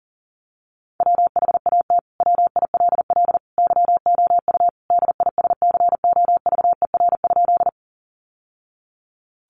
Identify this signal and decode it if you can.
Morse code, heard as a keyed tone, about 40 wpm, 700 Hz. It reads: WHAT WILL YOU DISCOVER?
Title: Sentences from Top 1100 Words 40wpm